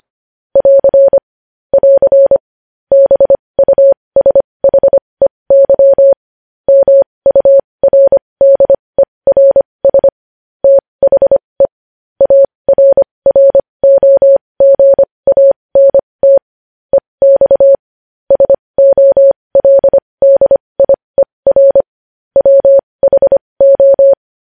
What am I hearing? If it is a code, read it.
Morse code, heard as a keyed tone, about 25 wpm, spaced as for 20 wpm, 550 Hz. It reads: <AR> <AR> BUSHEY MURDERS THE ARROGANT EX SOLDIER WHO
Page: news-headlines-in-morse-code-at-20-wpm